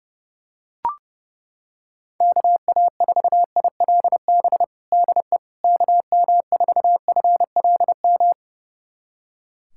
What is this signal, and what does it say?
Morse code, heard as a keyed tone, about 30 wpm, 700 Hz. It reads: KA4ILB DE KM4FLM
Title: USA Callsign DE Callsign 30wpm